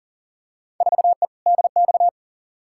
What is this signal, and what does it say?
Morse code, heard as a keyed tone, about 40 wpm, 700 Hz. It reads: 4E DX
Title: Field Day Exchanges Encore - 40wpm